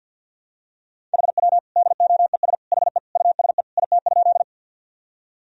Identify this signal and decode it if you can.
Morse code, heard as a keyed tone, about 51 wpm, 700 Hz. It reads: HW DOES HE USE IT?
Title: Sentences from Top 200 Words and Abbreviations 50wpm